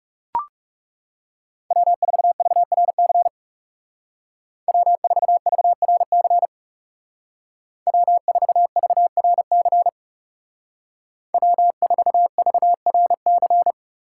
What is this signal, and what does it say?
Morse code, heard as a keyed tone, about 39 wpm, 700 Hz. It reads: W4VRC W4VRC W4VRC W4VRC